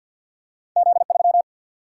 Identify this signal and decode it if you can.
Morse code, heard as a keyed tone, about 47 wpm, 700 Hz. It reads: Z3
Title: Single-Letter Single-Number 50wpm